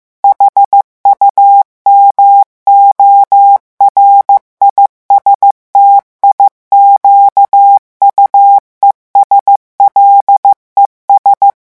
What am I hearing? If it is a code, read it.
Morse code, heard as a keyed tone, about 15 wpm, 800 Hz. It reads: HUMORISTIQUES LES
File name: morse mp3 / revision9 citations humoristiques vitesse 14 mots minute